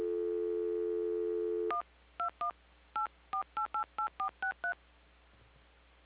DTMF, heard as telephone touch tones, tones of about 110 ms. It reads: 121 84888463